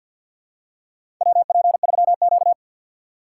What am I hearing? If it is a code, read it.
Morse code, heard as a keyed tone, about 51 wpm, 700 Hz. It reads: WP3Q
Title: USA Call Signs - 4 Characters - 2x Character Spacing - 50wpm